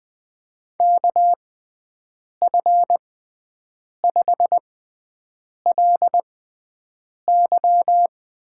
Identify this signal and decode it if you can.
Morse code, heard as a keyed tone, about 20 wpm, spaced as for 7 wpm, 700 Hz. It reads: KF5LY
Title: USA Call Signs - 6 Characters - 6x Character Spacing - 20wpm